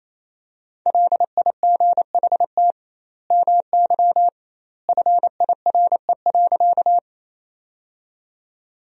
Morse code, heard as a keyed tone, about 28 wpm, 700 Hz. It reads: LIGHT MY FIRE.
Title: 2x Word-Spacing - Sentences from Top 400 Words - 28wpm